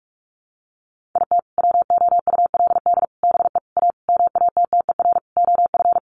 Morse code, heard as a keyed tone, about 44 wpm, 700 Hz. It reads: IT WOULD BE A MATTER OF
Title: Sentences from Top 1400 Words 45wpm